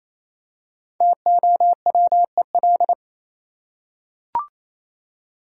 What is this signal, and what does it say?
Morse code, heard as a keyed tone, about 28 wpm, 700 Hz. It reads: TOWEL
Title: Five Letter Words - 28wpm